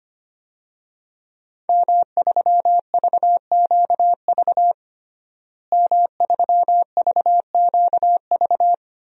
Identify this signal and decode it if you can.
Morse code, heard as a keyed tone, about 25 wpm, 700 Hz. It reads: M3VQV M3VQV
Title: DXCC-100 Call Signs - 5 Characters - Repeated 3x - 25wpm